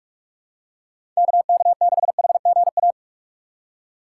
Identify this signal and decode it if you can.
Morse code, heard as a keyed tone, about 46 wpm, 700 Hz. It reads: KK6HGA
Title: USA Call Signs - 6 Characters - 4x Character Spacing - 45wpm